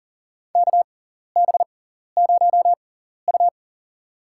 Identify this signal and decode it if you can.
Morse code, heard as a keyed tone, about 42 wpm, spaced as for 14 wpm, 690 Hz. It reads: KB0U
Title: USA Call Signs - 4 Characters - 6x Character Spacing - 40wpm